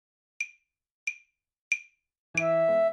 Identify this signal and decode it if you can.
There is no code.